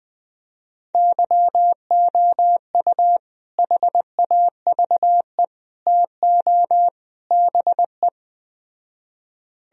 Morse code, heard as a keyed tone, about 20 wpm, 700 Hz. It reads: YOU HAVE TO BE
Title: Speed-Racing - Sets of 4 Words 17wpm